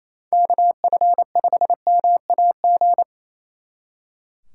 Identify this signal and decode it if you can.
Morse code, heard as a keyed tone, about 37 wpm, 700 Hz. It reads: KF5MAG